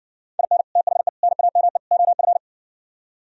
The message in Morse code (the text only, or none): IN THE NAME OF